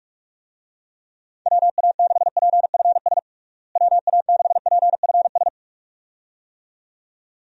Morse code, heard as a keyed tone, about 46 wpm, 700 Hz. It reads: WA6PFS WA6PFS
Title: USA Call Signs - 6 Characters - Repeated 2x - 45wpm